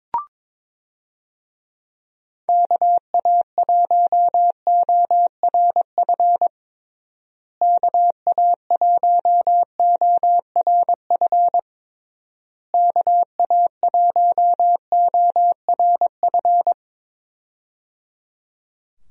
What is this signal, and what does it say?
Morse code, heard as a keyed tone, about 22 wpm, 700 Hz. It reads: KA1ORF KA1ORF KA1ORF
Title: USA Call Signs - 6 Characters - Repeated 3x - 22wpm